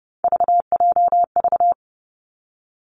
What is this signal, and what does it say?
Morse code, heard as a keyed tone, about 30 wpm, 700 Hz. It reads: VJV
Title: Three Challenging Letter-Number Combinations 30wpm